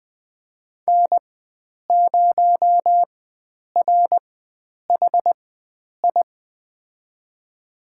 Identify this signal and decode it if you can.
Morse code, heard as a keyed tone, about 20 wpm, spaced as for 9 wpm, 700 Hz. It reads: N0RHI
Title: USA Call Signs - 5 Characters - 4x Character Spacing - 20wpm